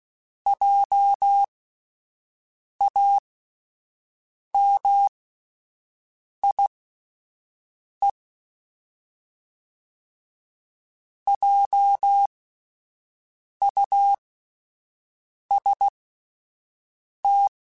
Morse code, heard as a keyed tone, about 16 wpm, spaced as for 5 wpm, 800 Hz. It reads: JAMIE JUST